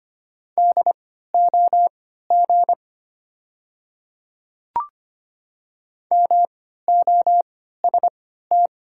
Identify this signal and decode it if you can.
Morse code, heard as a keyed tone, about 25 wpm, spaced as for 14 wpm, 690 Hz. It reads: DOG MOST
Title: 3x Character Spacing - Top 500 Words - 25wpm